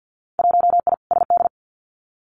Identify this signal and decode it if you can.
Morse code, heard as a keyed tone, about 50 wpm, 700 Hz. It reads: 1I SD
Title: Winter Field Day Exchanges 50wpm